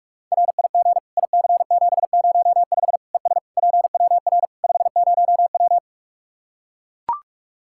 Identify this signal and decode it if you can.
Morse code, heard as a keyed tone, about 41 wpm, 700 Hz. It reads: RIG IC705 ES PWR 50W
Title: K1USN SST QSO Practice 45wpm